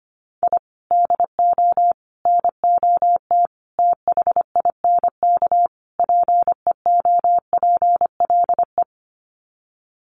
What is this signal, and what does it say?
Morse code, heard as a keyed tone, about 25 wpm, 700 Hz. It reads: I DO NOT THINK PEOPLE
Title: Sets of 5 Words Encore2 25wpm